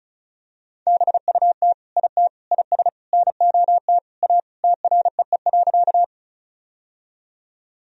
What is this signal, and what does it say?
Morse code, heard as a keyed tone, about 34 wpm, 700 Hz. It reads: BUT IT IS NOT A TREE.
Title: Sentences from Top 1000 Words - Review Part 1 - 35wpm